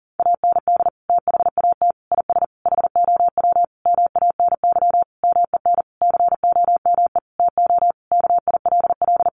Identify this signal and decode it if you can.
Morse code, heard as a keyed tone, about 37 wpm, 700 Hz. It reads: AND THAT IS HOW MANY MEN COME TO KILL
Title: Sentences from Top 1400 Words 40wpm